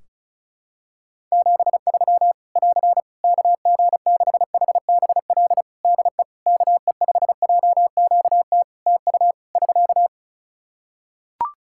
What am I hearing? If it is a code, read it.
Morse code, heard as a keyed tone, about 34 wpm, 700 Hz. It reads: 73 <AR> KG6HBL DE KE5JQT TU <SK>
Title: Practice QSO - Exchange 4 - Ending - 35wpm